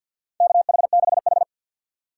Morse code, heard as a keyed tone, about 52 wpm, 700 Hz. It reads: KH6H
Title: USA Call Signs - 4 Characters - 2x Character Spacing - 50wpm